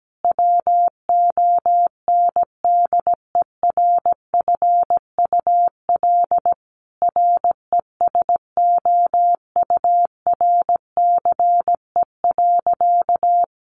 Morse code, heard as a keyed tone, about 17 wpm, 700 Hz. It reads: WONDERFUL RESOURCE.
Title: Sentences from Top 1300 Words 17wpm